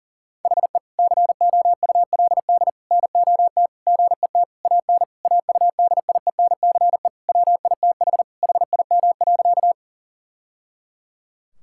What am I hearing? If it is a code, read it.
Morse code, heard as a keyed tone, about 37 wpm, 700 Hz. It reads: HE COULD NOT GET AN AUDIENCE WITH HIM.
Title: Sentences from Top 1000 Words - Review Part 5 - 40wpm